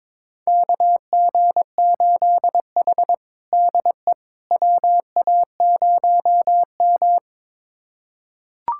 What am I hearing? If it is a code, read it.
Morse code, heard as a keyed tone, about 22 wpm, 700 Hz. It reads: KG8H DE WA0M
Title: USA Callsign DE Callsign - 4 Characters - 22wpm